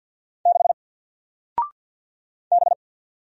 Morse code, heard as a keyed tone, about 49 wpm, 700 Hz. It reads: 6 B
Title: Single Letter-Number Alt 50wpm